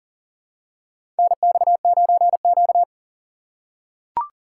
Morse code, heard as a keyed tone, about 40 wpm, 690 Hz. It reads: NX9Q
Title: USA Call Signs - 3x Character Spacing - 40wpm